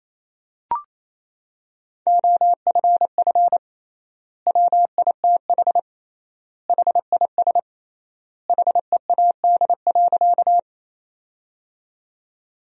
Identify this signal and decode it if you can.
Morse code, heard as a keyed tone, about 28 wpm, 700 Hz. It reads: OFF WITH HIS HEAD.
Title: 3x Word-Spacing - Sentences from Top 300 Words - 28wpm